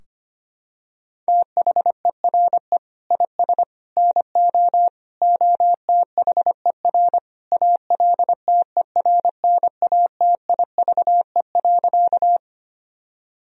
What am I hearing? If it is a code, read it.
Morse code, heard as a keyed tone, about 25 wpm, 700 Hz. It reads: THERE IS NO OTHER ALTERNATIVE.
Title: Sentences from Top 1000 Words - Review Part 4 - 25wpm